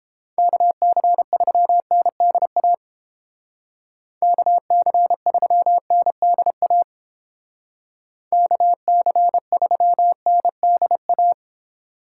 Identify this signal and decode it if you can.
Morse code, heard as a keyed tone, about 29 wpm, 700 Hz. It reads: KC3NDA KC3NDA KC3NDA